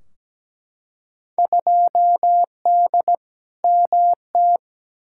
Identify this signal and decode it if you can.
Morse code, heard as a keyed tone, about 17 wpm, 700 Hz. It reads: 2D MT
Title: Field Day Exchanges Encore - 17wpm